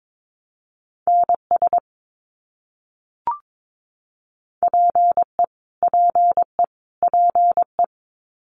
Morse code, heard as a keyed tone, about 22 wpm, 700 Hz. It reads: NS PE PE PE
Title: Canadian Provinces and Territories - Repeated 3x - 22wpm